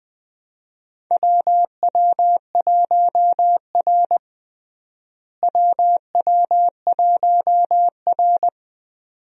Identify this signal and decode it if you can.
Morse code, heard as a keyed tone, about 20 wpm, 700 Hz. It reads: WW1R WW1R
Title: USA Call Signs - 4 Characters - Repeated 2x - 20wpm